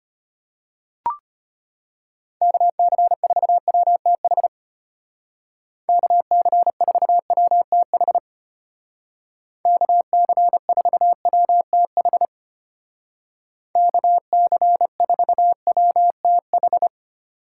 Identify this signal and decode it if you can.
Morse code, heard as a keyed tone, about 31 wpm, 700 Hz. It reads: KC4WTH KC4WTH KC4WTH KC4WTH